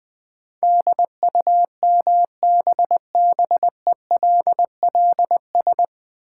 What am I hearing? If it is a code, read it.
Morse code, heard as a keyed tone, about 20 wpm, 700 Hz. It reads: DUMBBELLS
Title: Easy Words with Hints - 20wpm